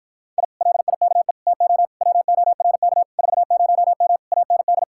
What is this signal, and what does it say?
Morse code, heard as a keyed tone, about 52 wpm, 700 Hz. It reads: I LIKE TO WORK 40M AND